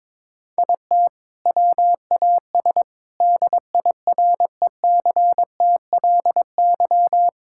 Morse code, heard as a keyed tone, about 22 wpm, 700 Hz. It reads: IT WAS DIRECTLY
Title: Sentences from Top 1000 Words - Review Part 5 - 22wpm